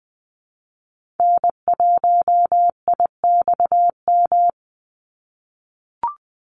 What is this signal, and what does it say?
Morse code, heard as a keyed tone, about 19 wpm, 700 Hz. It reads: N1IXM
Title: USA Call Signs 20wpm